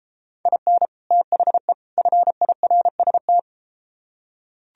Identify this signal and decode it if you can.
Morse code, heard as a keyed tone, about 32 wpm, 700 Hz. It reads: IN THE FIRST